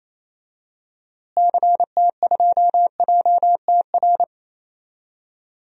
Morse code, heard as a keyed tone, about 27 wpm, 700 Hz. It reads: CT2JTR
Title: DXCC-100 Call Signs - 6 Characters - Repeated 3x - 28wpm